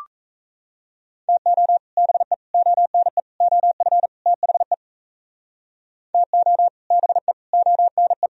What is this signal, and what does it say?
Morse code, heard as a keyed tone, about 38 wpm, 700 Hz. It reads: TO BE ONE OF THE TO BE ONE